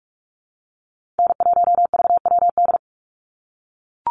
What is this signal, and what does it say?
Morse code, heard as a keyed tone, about 44 wpm, 700 Hz. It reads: N1VWD
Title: NA SOTA Chaser Callsigns 45wpm